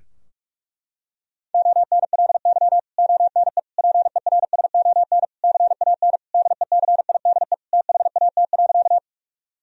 Morse code, heard as a keyed tone, about 46 wpm, 700 Hz. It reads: ONLY ONE PERSON CAN DECIDE THAT.
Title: Sentences from Top 1000 Words - Review Part 3 - 45wpm